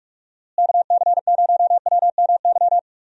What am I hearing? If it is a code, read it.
Morse code, heard as a keyed tone, about 45 wpm, 700 Hz. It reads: KC0WMY
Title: USA Call Signs - 6 Characters - Repeated 2x - 45wpm